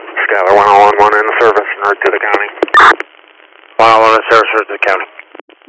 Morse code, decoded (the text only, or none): none